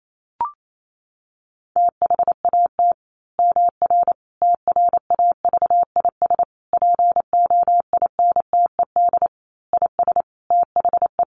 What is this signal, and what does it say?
Morse code, heard as a keyed tone, about 28 wpm, 700 Hz. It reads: THAT MR TRAVIS POINTED IS THE
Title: A Sound Of Thunder 28wpm